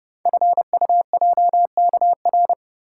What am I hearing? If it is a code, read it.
Morse code, heard as a keyed tone, about 28 wpm, 700 Hz. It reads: FUJKR
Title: Code Groups - 5 Letters - 30wpm